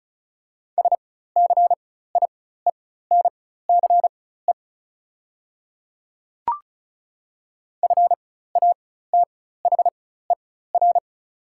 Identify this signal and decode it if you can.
Morse code, heard as a keyed tone, about 36 wpm, spaced as for 16 wpm, 700 Hz. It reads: SCIENCE FATHER
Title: 4x Character Spacing - Top 500 Words - 35wpm